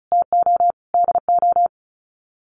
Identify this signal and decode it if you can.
Morse code, heard as a keyed tone, about 34 wpm, 700 Hz. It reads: TO DO
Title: Sets of 4 Words 35wpm